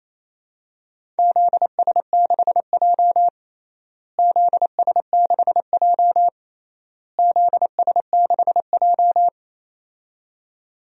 Morse code, heard as a keyed tone, about 28 wpm, 700 Hz. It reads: ZS6J ZS6J ZS6J
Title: DXCC-100 Call Signs - 4 Characters - Repeated 3x - 28wpm